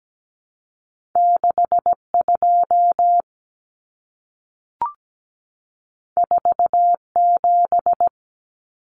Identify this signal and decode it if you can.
Morse code, heard as a keyed tone, about 17 wpm, 700 Hz. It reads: 62 47
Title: Numbers - 2-Digit 17wpm